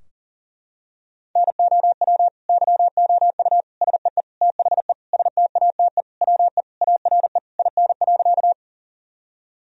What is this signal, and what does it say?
Morse code, heard as a keyed tone, about 40 wpm, 700 Hz. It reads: NOW YOU SEE THE STATE WE ARE IN.
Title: Sentences from Top 1000 Words - Review Part 2 - 40wpm